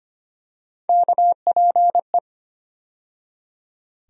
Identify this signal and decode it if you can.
Morse code, heard as a keyed tone, about 25 wpm, 700 Hz. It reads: KPE
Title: USA Call Sign Postfixes - 3 Letters 25wpm